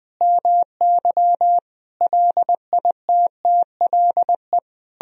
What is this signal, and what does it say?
Morse code, heard as a keyed tone, about 20 wpm, 700 Hz. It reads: MY LITTLE
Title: Sentences from Top 1000 Words - Review Part 4 - 20wpm